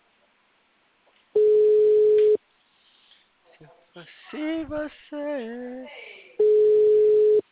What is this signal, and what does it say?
Ringback tone (Europe, 425 Hz), with speech.